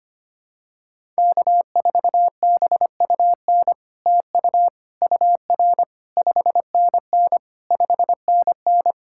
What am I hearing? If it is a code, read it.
Morse code, heard as a keyed tone, about 25 wpm, 700 Hz. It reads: K4BUN TU UR 5NN 5NN
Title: SOTA QSO Practice 25wpm